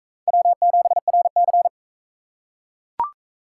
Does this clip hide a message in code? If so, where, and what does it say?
Yes: Morse code from 0.3 to 1.7 s, heard as a keyed tone, about 40 wpm, 700 Hz. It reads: W7RC